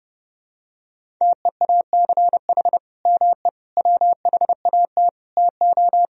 Morse code, heard as a keyed tone, about 30 wpm, 700 Hz. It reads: TEACH ME WHAT TO
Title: Sentences from Top 500 Words and Abbreviations 30wpm